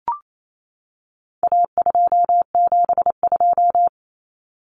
Morse code, heard as a keyed tone, about 28 wpm, 700 Hz. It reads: A272
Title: Common Cut Numbers - 4-Digit 28wpm